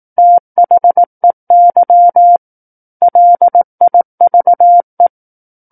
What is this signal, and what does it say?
Morse code, heard as a keyed tone, about 18 wpm, 700 Hz. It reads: THEY LIVE
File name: 18 WPM